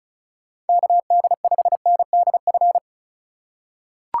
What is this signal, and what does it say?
Morse code, heard as a keyed tone, about 34 wpm, 700 Hz. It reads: KD5NDF
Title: USA Call Signs - 6 Characters - 4x Character Spacing - 35wpm